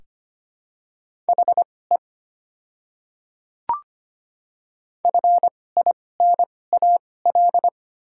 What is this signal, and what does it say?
Morse code, heard as a keyed tone, about 25 wpm, spaced as for 18 wpm, 690 Hz. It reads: HE FINAL
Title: 2x Character Spacing - Top 500 Words - 25wpm